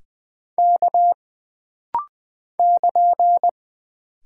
Morse code, heard as a keyed tone, about 20 wpm, 700 Hz. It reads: K <KN>
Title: Single Character (Letter:Number:Punctuation:Prosign) - Rapid-Fire - 20wpm